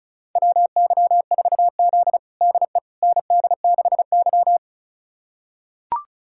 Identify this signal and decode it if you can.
Morse code, heard as a keyed tone, about 34 wpm, 700 Hz. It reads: WY4Z DE ND6Y
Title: USA Callsign DE Callsign - 4 Characters - 35wpm